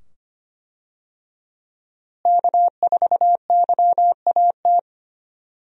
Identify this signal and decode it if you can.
Morse code, heard as a keyed tone, about 25 wpm, 700 Hz. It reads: K4YAT
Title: USA Call Signs - 5 Characters - Repeated 2x - 25wpm